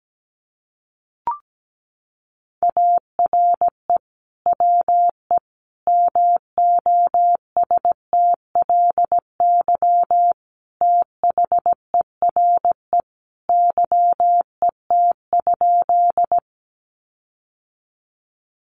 Morse code, heard as a keyed tone, about 17 wpm, 700 Hz. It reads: ARE WE MOSTLY THERE YET?
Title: Sentences from Top 1300 Words 17wpm